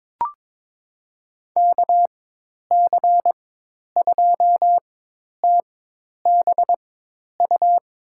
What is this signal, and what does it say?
Morse code, heard as a keyed tone, about 22 wpm, spaced as for 10 wpm, 700 Hz. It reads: KC2TBU
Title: USA Call Signs - 6 Characters - 4x Character Spacing - 22wpm